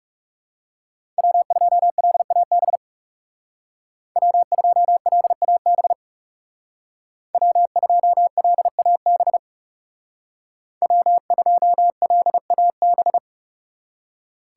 Morse code, heard as a keyed tone, about 39 wpm, 700 Hz. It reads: W2LAB W2LAB W2LAB W2LAB